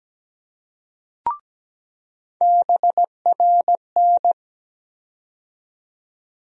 Morse code, heard as a keyed tone, about 17 wpm, 700 Hz. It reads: BRN
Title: USA Call Sign Postfixes - 3 Letters 17wpm